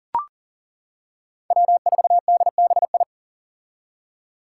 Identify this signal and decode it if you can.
Morse code, heard as a keyed tone, about 38 wpm, 700 Hz. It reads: W4DBI
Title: Five Challenging Letter-Number Combinations 40wpm